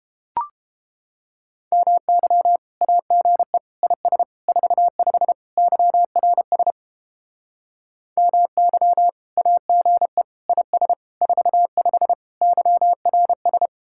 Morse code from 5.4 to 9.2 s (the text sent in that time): YRS MY